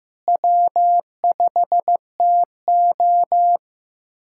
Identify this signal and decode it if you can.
Morse code, heard as a keyed tone, about 15 wpm, 700 Hz. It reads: W5TO
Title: USA Callsign DE Callsign - 4 Characters - 15wpm